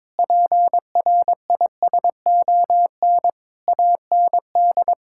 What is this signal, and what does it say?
Morse code, heard as a keyed tone, about 22 wpm, 700 Hz. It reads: PRISON AND
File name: Sets of 9 Words - 22wpm